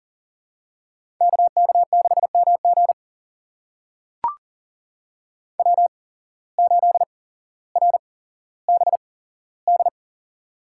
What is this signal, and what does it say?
Morse code, heard as a keyed tone, about 40 wpm, spaced as for 17 wpm, 700 Hz. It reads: KK6MG W8RBD
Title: USA Call Signs - 5 Characters - 8x Character Spacing - 40wpm